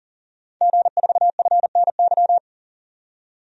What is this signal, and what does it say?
Morse code, heard as a keyed tone, about 40 wpm, 700 Hz. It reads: G4FNY
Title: DXCC-100 Call Signs - 5 Characters - 40wpm